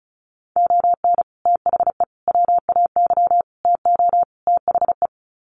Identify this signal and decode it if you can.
Morse code, heard as a keyed tone, about 34 wpm, 700 Hz. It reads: ON THE WAY TO THE
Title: Sets of 5 Words 35wpm